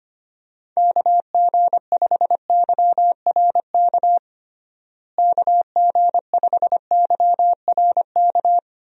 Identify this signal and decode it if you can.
Morse code, heard as a keyed tone, about 25 wpm, 700 Hz. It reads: KG5YRK KG5YRK
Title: USA Call Signs - 6 Characters - Repeated 3x - 25wpm